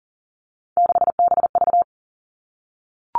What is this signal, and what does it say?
Morse code, heard as a keyed tone, about 42 wpm, 700 Hz. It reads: 6BV
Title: Three Challenging Letter-Number Combinations 40wpm